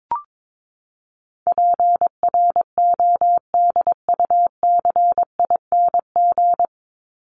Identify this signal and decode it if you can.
Morse code, heard as a keyed tone, about 22 wpm, 700 Hz. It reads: PRODUCING
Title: ING Suffix - Top 100 Words with Base Words 22wpm